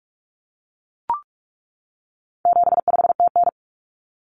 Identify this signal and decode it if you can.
Morse code, heard as a keyed tone, about 44 wpm, 700 Hz. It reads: 75TN
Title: Common Cut Numbers - 4-Digit 45wpm